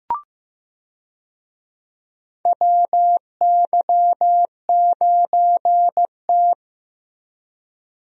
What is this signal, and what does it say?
Morse code, heard as a keyed tone, about 15 wpm, 700 Hz. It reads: WY9T
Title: USA Call Signs - 4 Characters - Repeated 3x - 15wpm